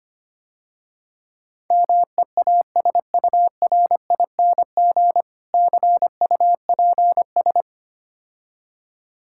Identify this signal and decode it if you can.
Morse code, heard as a keyed tone, about 25 wpm, 700 Hz. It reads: MEASURING CUPS
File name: Easy Words - Speed-Racing with Hints - 25wpm